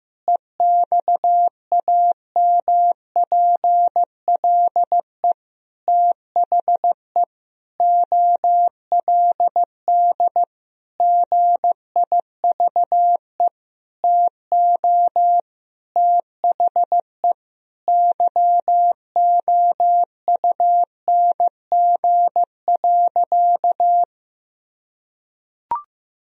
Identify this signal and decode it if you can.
Morse code, heard as a keyed tone, about 16 wpm, 700 Hz. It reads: EXAMPLE THE OLD GIVE TO THE YOUNG.
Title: Sentences from Top 1000 Words - Review Part 1 - 15wpm